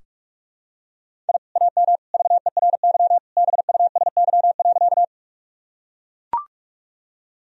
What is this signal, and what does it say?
Morse code, heard as a keyed tone, about 41 wpm, 700 Hz. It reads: I AM VERY BUSY.
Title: Sentences from Top 1000 Words - Review Part 3 - 45wpm